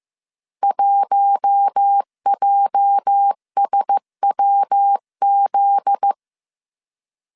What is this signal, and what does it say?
Morse code, heard as a keyed tone, about 15 wpm, 800 Hz. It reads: 1JSWZ
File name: CW / DET / 1 9 1500C S A 14wpm 3222